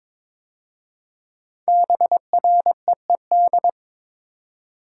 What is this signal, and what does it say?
Morse code, heard as a keyed tone, about 22 wpm, 700 Hz. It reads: BREED
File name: 5 Letter Words - First 3 Letters Given - 22wpm